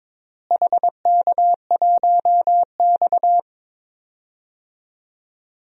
Morse code, heard as a keyed tone, about 22 wpm, 700 Hz. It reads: HK1X
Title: DXCC-100 Call Signs - 4 Characters - Repeated 3x - 22wpm